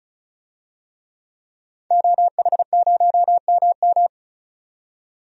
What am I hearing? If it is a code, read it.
Morse code, heard as a keyed tone, about 36 wpm, 700 Hz. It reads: OH0MM
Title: DXCC-100 Call Signs - 5 Characters - 35wpm